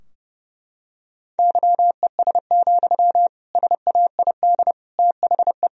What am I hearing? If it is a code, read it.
Morse code, heard as a keyed tone, about 30 wpm, 700 Hz. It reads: YES, SAID THE
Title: Sentences from Top 1000 Words - Review Part 4 - 30wpm